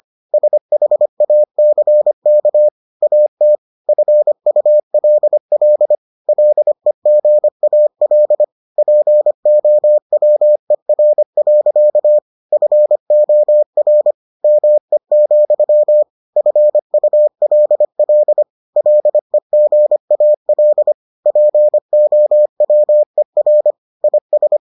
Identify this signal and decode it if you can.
Morse code, heard as a keyed tone, about 25 wpm, 600 Hz. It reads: SHACK AT FULL LEGAL POWER. FOR ME, FULL LEGAL POWER IS